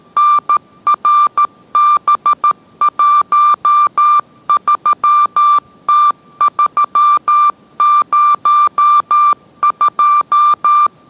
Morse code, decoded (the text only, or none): NRB13T302